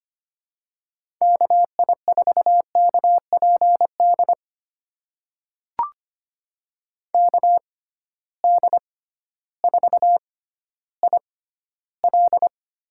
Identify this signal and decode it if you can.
Morse code, heard as a keyed tone, about 25 wpm, spaced as for 14 wpm, 700 Hz. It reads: KI4KPD KD4IL
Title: USA Call Signs - 6 Characters - 6x Character Spacing - 25wpm